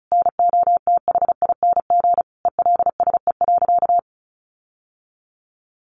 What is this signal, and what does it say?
Morse code, heard as a keyed tone, about 34 wpm, 700 Hz. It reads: NOTHING ELSE.
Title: Sentences from Top 1400 Words 35wpm